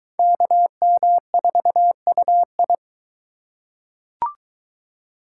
Morse code, heard as a keyed tone, about 23 wpm, 700 Hz. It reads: KM4UI